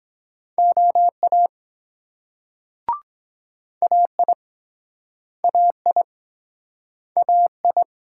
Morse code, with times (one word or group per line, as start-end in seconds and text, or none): 0.6-1.5 OA
3.8-4.3 AI
5.4-6.0 AI
7.2-7.8 AI